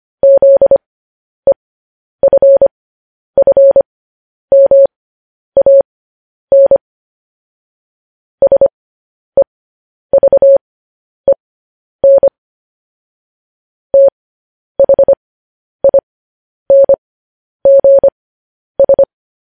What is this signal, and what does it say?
Morse code, heard as a keyed tone, about 25 wpm, spaced as for 10 wpm, 550 Hz. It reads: ZEFFMAN SEVEN THINGS
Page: news-headlines-in-morse-code-at-10-wpm